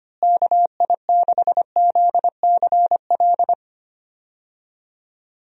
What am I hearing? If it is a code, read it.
Morse code, heard as a keyed tone, about 25 wpm, 700 Hz. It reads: KI6ZCL
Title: USA Call Signs - 6 Characters - Repeated 2x - 25wpm